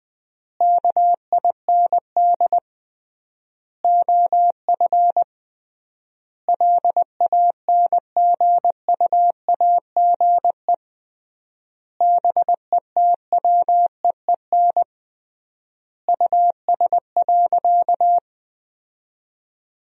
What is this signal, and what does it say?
Morse code, heard as a keyed tone, about 21 wpm, 700 Hz. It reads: KIND OF LANGUAGE BETWEEN US.
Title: 3x Word-Spacing - Sentences from Top 500 Words - 20wpm